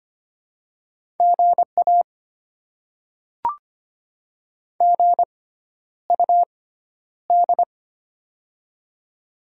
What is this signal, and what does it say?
Morse code, heard as a keyed tone, about 25 wpm, spaced as for 11 wpm, 700 Hz. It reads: GA GUD
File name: Top 20 QSO Elements - 6x Character Spacing - 25wpm